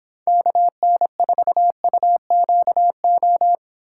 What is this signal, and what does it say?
Morse code, heard as a keyed tone, about 26 wpm, 700 Hz. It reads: KN4UQO